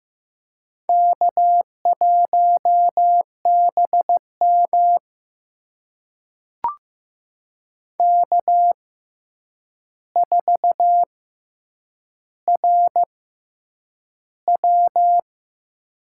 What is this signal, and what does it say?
Morse code, heard as a keyed tone, about 15 wpm, spaced as for 8 wpm, 700 Hz. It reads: K1BM K4RW
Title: USA Call Signs - 4 Characters - 6x Character Spacing - 15wpm